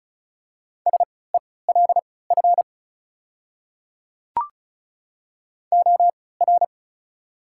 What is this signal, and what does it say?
Morse code, heard as a keyed tone, about 34 wpm, spaced as for 20 wpm, 690 Hz. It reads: SELF OR
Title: 3x Character Spacing - Top 300 Words - 35wpm